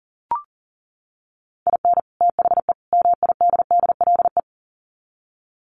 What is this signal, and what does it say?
Morse code, heard as a keyed tone, about 40 wpm, 700 Hz. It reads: IN THE MIDDLE
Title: Sets of 3 Words 40wpm